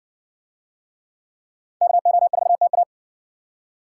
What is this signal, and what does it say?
Morse code, heard as a keyed tone, about 60 wpm, 700 Hz. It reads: KO4TA